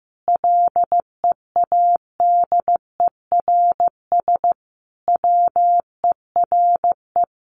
Morse code, heard as a keyed tone, about 15 wpm, 700 Hz. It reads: LEADERS WERE
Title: A Sound Of Thunder 15wpm